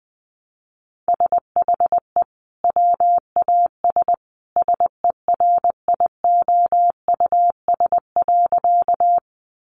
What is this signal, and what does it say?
Morse code, heard as a keyed tone, about 20 wpm, 700 Hz. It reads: SHE WAS SERIOUS.
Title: Sentences from Top 1100 Words 20wpm